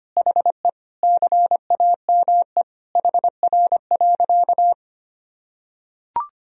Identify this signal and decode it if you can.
Morse code, heard as a keyed tone, about 25 wpm, 700 Hz. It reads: HE CAME HR.
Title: Sentences from Top 500 Words and Abbreviations 25wpm